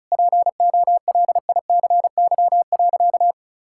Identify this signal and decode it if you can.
Morse code, heard as a keyed tone, about 34 wpm, 700 Hz. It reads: POLICY.
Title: Sentences from Top 1000 Words - Review Part 5 - 35wpm